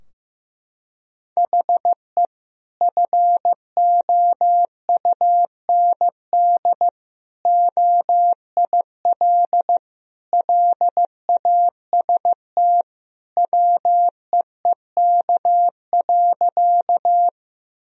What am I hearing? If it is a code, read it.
Morse code, heard as a keyed tone, about 15 wpm, 700 Hz. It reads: HE FOUND OIL LAST WEEK.
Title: Sentences from Top 1000 Words - Review Part 4 - 15wpm